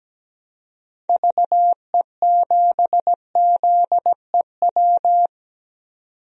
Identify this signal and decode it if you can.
Morse code, heard as a keyed tone, about 17 wpm, 700 Hz. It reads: VE7ZEW
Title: DXCC-100 Call Signs - 6 Characters - 17wpm